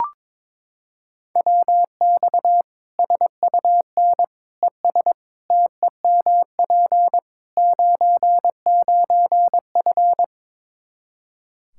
Morse code, heard as a keyed tone, about 21 wpm, 700 Hz. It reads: WX SUN ES TEMP 99F
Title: Practice QSO - Exchange 2 - Info - 22wpm